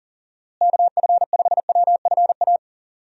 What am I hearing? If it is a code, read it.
Morse code, heard as a keyed tone, about 40 wpm, 700 Hz. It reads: KF5WFA
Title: USA Call Signs - 6 Characters - Repeated 2x - 40wpm